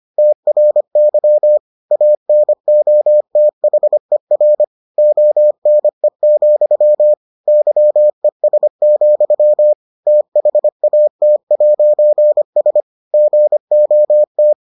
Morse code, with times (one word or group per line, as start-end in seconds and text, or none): 0.2-1.6 TRY
1.9-4.6 ANOTHER
5.0-7.1 ONE,
7.5-9.7 YES,
10.1-12.8 THAT'S
13.1-14.5 GOT